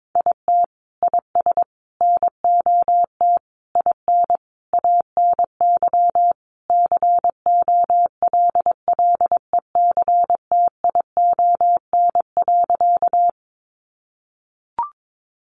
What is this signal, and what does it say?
Morse code, heard as a keyed tone, about 22 wpm, 700 Hz. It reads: IT IS NOT IN ANY COLLECTION.
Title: Sentences from Top 1200 Words 22wpm